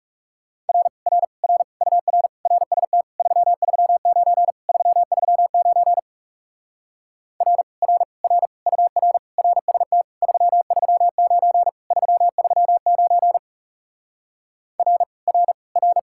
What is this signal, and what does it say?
Morse code, heard as a keyed tone, about 40 wpm, 700 Hz. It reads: R R R UR RST 339 339 R R R UR RST 339 339 R R R